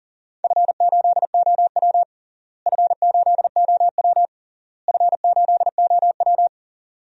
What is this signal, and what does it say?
Morse code, heard as a keyed tone, about 40 wpm, 700 Hz. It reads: F8OW F8OW F8OW
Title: DXCC-100 Call Signs - 4 Characters - Repeated 3x - 40wpm